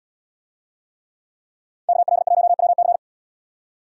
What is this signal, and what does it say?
Morse code, heard as a keyed tone, about 73 wpm, 700 Hz. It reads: KL1WC